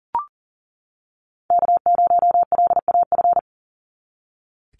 Morse code, heard as a keyed tone, about 40 wpm, 700 Hz. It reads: K0LAF
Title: NA SOTA Chaser Callsigns 40wpm